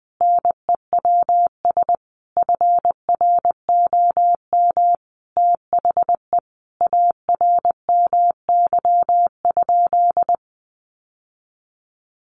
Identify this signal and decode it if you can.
Morse code, heard as a keyed tone, about 20 wpm, 700 Hz. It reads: NEWS FROM THE ARMY?
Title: Sentences from Top 800 Words 20wpm